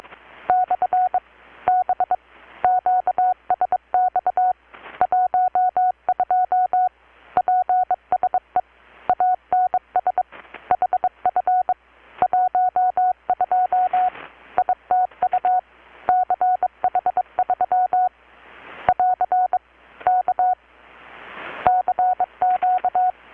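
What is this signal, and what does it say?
Morse code, heard as a keyed tone, about 22 wpm, 720 Hz. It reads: / B QSX 12 PSE ANS HF 12 ITU CH3 <AR> K CQ
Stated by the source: Morse from KPH's Night of Nights XX maritime radiotelegraphy event, received in Chiapas, Mexico about 3,800 km away on a Kenwood R-600 with a grounded 10 m long-wire antenna.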